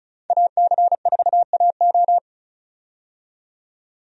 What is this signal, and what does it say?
Morse code, heard as a keyed tone, about 36 wpm, 700 Hz. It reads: AC4AO
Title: USA Call Signs - 5 Characters - 35wpm